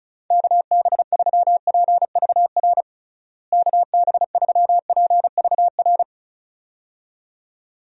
Morse code, heard as a keyed tone, about 34 wpm, 700 Hz. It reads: KB3PVR KB3PVR
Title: USA Call Signs - 6 Characters - Repeated 2x - 35wpm